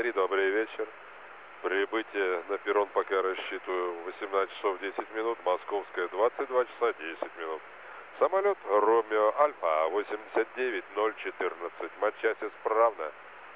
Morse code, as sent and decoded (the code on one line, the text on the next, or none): none